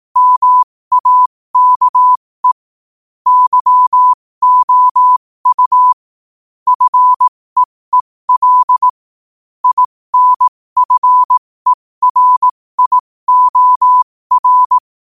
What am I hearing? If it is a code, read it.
Morse code, heard as a keyed tone, about 18 wpm, spaced as for 15 wpm, 1000 Hz. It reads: MAKE YOU FEEL INFERIOR
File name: qotdfiles / 15wpm